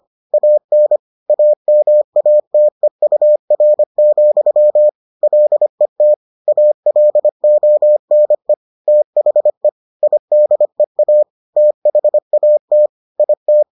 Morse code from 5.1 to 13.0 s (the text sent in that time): LET ALONE THE IDEA THAT